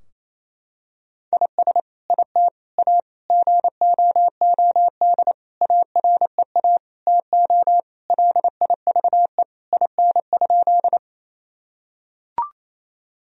Morse code, heard as a keyed tone, about 28 wpm, 700 Hz. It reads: IS IT A GOOD AREA TO LIVE IN?
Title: Sentences from Top 1000 Words - Review Part 2 - 28wpm